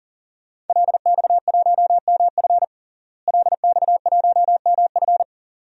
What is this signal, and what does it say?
Morse code, heard as a keyed tone, about 40 wpm, 700 Hz. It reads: LX1MF LX1MF
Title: DXCC-100 Call Signs - 5 Characters - Repeated 2x - 40wpm